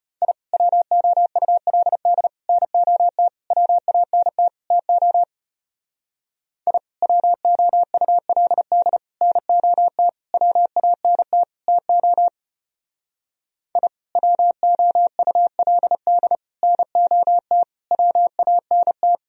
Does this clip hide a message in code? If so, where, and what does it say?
Yes: Morse code from 0.2 to 19.2 s, heard as a keyed tone, about 33 wpm, 700 Hz. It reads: I WOULD NOT WANT TO I WOULD NOT WANT TO I WOULD NOT WANT